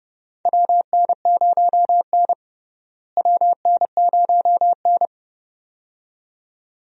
Morse code, heard as a keyed tone, about 32 wpm, 700 Hz. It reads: WN0N WN0N
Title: USA Call Signs - 4 Characters - Repeated 2x - 30wpm